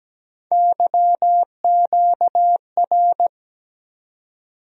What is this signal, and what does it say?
Morse code, heard as a keyed tone, about 17 wpm, 700 Hz. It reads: YQR
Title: USA Call Sign Postfixes - 3 Letters 17wpm